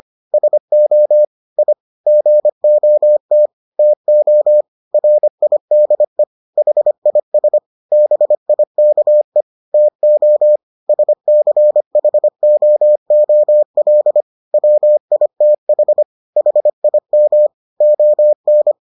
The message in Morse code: SO I GOT TO RIDE HIS BIKE TO SCHOOL WITH HIM ON

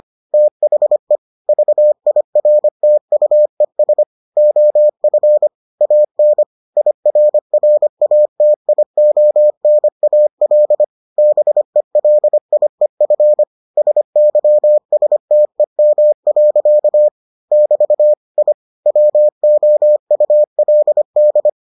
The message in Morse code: THE VIRTUES OF AN IRRATIONAL BELIEF SYSTEM. <BT> I WOULD

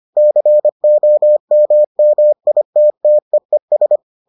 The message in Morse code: COMMITTEES

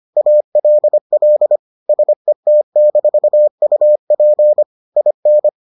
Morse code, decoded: ALL SET-UP IN

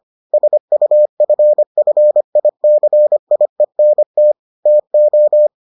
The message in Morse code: SUFFICIENT TO

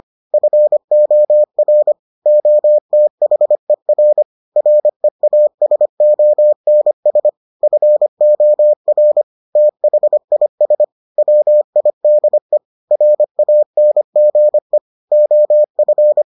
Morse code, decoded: FOR OTHER REASONS FOR THIS WIDE RANGE OF

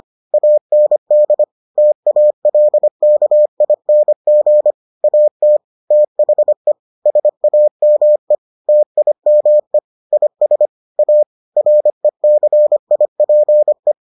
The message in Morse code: AND TALKING AT THE SAME TIME IS A RECIPE

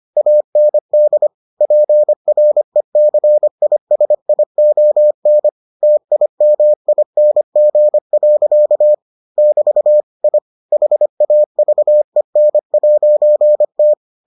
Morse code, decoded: AND PRECISION TIMING. <BT> I HAVEN'T

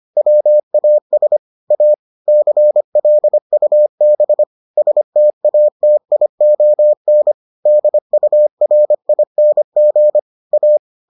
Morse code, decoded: WAS A CLUB STATION DURING A

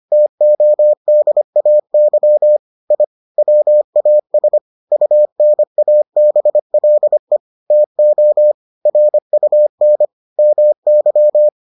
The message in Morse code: TODAY I WAS UNABLE TO RUN MY

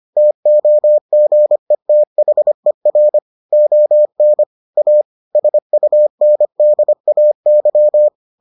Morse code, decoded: TOGETHER ON A SUNDAY

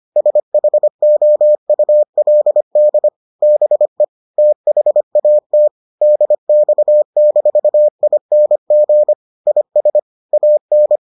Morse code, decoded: SHOULD BE THAT DX-ING IS AN

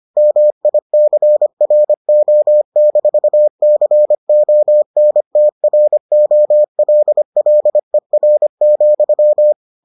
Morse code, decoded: MICRO-CONTROLLER,

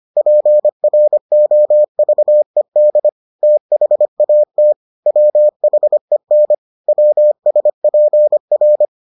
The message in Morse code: PROVED THAT WHEN WSPR

